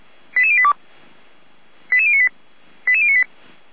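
Five-tone selective call (ZVEI), heard as quick runs of five tones, tones of about 70 ms. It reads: 88082 88098 88098
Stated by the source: ZVEI five-tone selective call received by SDR, code 88098.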